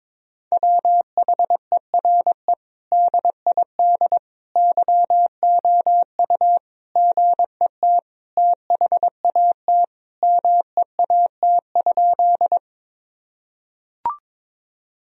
Morse code, heard as a keyed tone, about 22 wpm, 700 Hz. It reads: WHERE DID YOU GET THAT MEAT?
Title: Sentences from Top 1000 Words - Review Part 3 - 22wpm